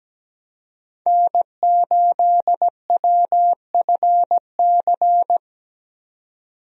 Morse code, heard as a keyed tone, about 17 wpm, 700 Hz. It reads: N8WFC
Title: USA Call Signs - 5 Characters - 17wpm